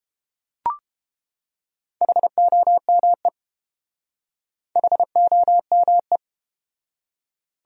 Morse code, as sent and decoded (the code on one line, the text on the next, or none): .... --- -- . / .... --- -- .
HOME HOME